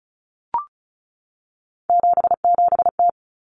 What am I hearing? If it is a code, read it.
Morse code, heard as a keyed tone, about 34 wpm, 700 Hz. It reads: 77T
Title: Common Cut Numbers - 3-Digit 35wpm